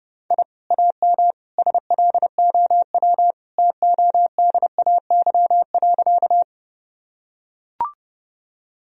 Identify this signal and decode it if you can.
Morse code, heard as a keyed tone, about 30 wpm, 700 Hz. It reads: I AM SLOW TODAY.
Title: Sentences from Top 1000 Words - Review Part 2 - 30wpm